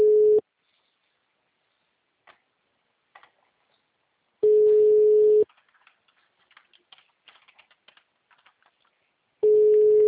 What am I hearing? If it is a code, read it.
Ringback tone (Europe, 425 Hz).
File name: out- -2004- - - .8